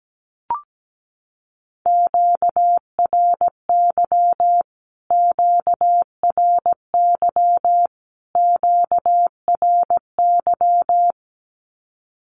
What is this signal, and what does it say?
Morse code, heard as a keyed tone, about 17 wpm, 700 Hz. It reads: QRY QRY QRY
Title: Q-Codes Repeated 3x 17wpm